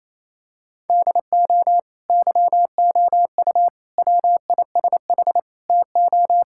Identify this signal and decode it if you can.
Morse code, heard as a keyed tone, about 28 wpm, 700 Hz. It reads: DO YOU WISH TO
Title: Sentences from Top 1600 Words 28wpm